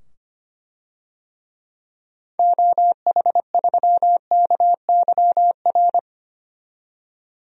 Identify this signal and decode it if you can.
Morse code, heard as a keyed tone, about 25 wpm, 700 Hz. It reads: OH3KYR
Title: DXCC-100 Call Signs - 6 Characters - 25wpm